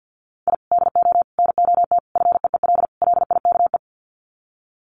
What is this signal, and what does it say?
Morse code, heard as a keyed tone, about 50 wpm, 700 Hz. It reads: I DO NOT FEEL LIKE
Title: Sets of 5 Words Encore 50wpm